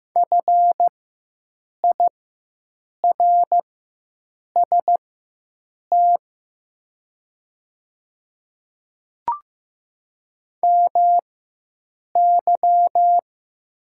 Morse code, heard as a keyed tone, about 15 wpm, spaced as for 7 wpm, 700 Hz. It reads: FIRST MY
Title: 4x Character Spacing - Top 300 Words - 15wpm